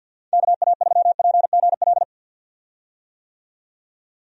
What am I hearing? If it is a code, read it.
Morse code, heard as a keyed tone, about 52 wpm, 700 Hz. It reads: KA3PGL
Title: USA Call Signs - 6 Characters - 50wpm